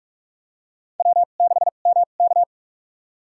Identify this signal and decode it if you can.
Morse code, heard as a keyed tone, about 45 wpm, spaced as for 32 wpm, 700 Hz. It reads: W6MK
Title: USA Call Signs - 4 Characters - 2x Character Spacing - 45wpm